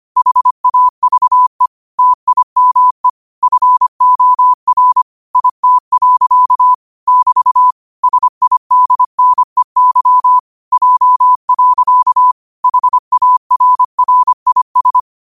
Morse code, heard as a keyed tone, about 25 wpm, 1000 Hz. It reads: HAVE TIME FOR IT. <BT> SIDNEY J. HARRIS